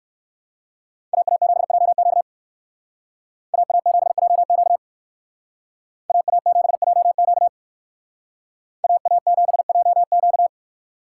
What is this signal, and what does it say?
Morse code, heard as a keyed tone, about 58 wpm, 700 Hz. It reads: AA7JQ AA7JQ AA7JQ AA7JQ